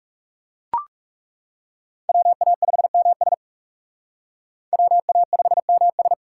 Morse code, heard as a keyed tone, about 36 wpm, 700 Hz. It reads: WA5MS WA5MS
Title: Speed Racing - USA 2x2 Call Signs 30wpm